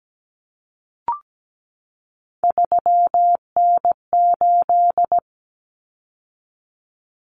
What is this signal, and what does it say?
Morse code, heard as a keyed tone, about 17 wpm, 700 Hz. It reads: 3N8
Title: Common Cut Numbers - 3-Digit 17wpm